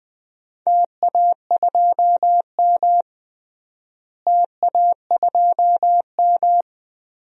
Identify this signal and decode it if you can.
Morse code, heard as a keyed tone, about 21 wpm, 700 Hz. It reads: TA2M TA2M
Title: DXCC-100 Call Signs - 4 Characters - Repeated 3x - 20wpm